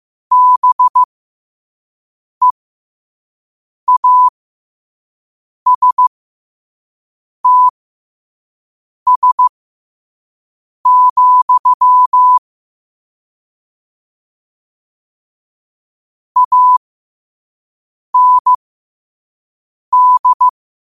Morse code, heard as a keyed tone, about 15 wpm, spaced as for 5 wpm, 1000 Hz. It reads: BEASTS, AND
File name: qotdfiles / 5wpm